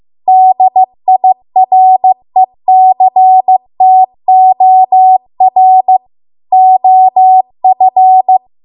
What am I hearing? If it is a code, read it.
Morse code, heard as a keyed tone, about 15 wpm, 750 Hz. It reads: DIRECTOR OF